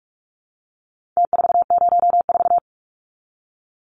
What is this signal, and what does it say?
Morse code, heard as a keyed tone, about 44 wpm, 700 Hz. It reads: T404
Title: Numbers - Contest Style - 4-Digit 45wpm